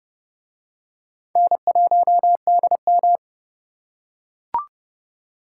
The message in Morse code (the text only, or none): N1DM